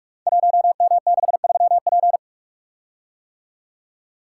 Morse code, heard as a keyed tone, about 44 wpm, 690 Hz. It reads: 1M63P
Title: Code Groups - 5 Letter:Numbers - 45wpm